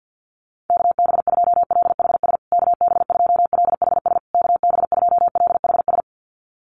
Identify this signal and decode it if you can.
Morse code, heard as a keyed tone, about 49 wpm, 700 Hz. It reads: KB2LHS KB2LHS KB2LHS
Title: USA Call Signs Repeated 3x 50wpm